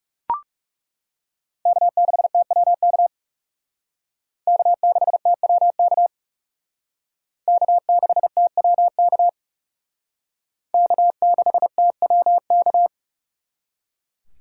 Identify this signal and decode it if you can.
Morse code, heard as a keyed tone, about 37 wpm, 700 Hz. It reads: K6TWK K6TWK K6TWK K6TWK
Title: Speed Racing - USA 1x3 Call Signs 30wpm